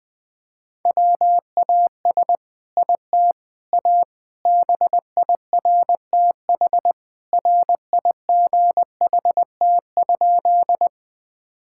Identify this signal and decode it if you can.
Morse code, heard as a keyed tone, about 20 wpm, 700 Hz. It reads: WAS IT A BIRTH RIGHT?
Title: Sentences from Top 1000 Words - Review Part 5 - 20wpm